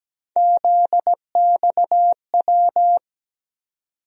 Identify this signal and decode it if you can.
Morse code, heard as a keyed tone, about 17 wpm, 700 Hz. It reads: ZXW
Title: USA Call Sign Postfixes - 3 Letters 17wpm